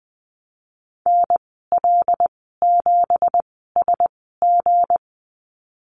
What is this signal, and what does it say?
Morse code, heard as a keyed tone, about 20 wpm, 700 Hz. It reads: N L 7 S G
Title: 238 Lesson 38g - Letter q - Call Signs 20wpm